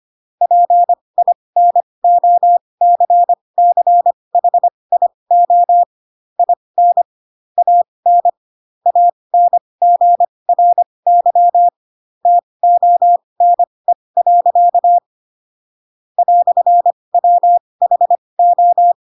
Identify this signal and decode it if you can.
Morse code, heard as a keyed tone, about 25 wpm, spaced as for 20 wpm, 700 Hz. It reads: PINOCCHIO IN AN ANGRY TONE. "WHO
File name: Pinocchio / Pinocchio ch7 15 25 ik1hge